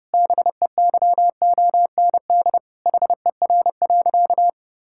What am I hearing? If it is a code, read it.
Morse code, heard as a keyed tone, about 30 wpm, 700 Hz. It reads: BEYOND HER.
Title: Sentences from Top 1000 Words - Review Part 6 - 30wpm